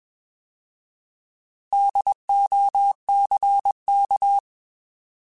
Morse code, heard as a keyed tone, about 21 wpm, 800 Hz. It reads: DOCK
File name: Alphabetical / Alphabetical mp3s / Ds @20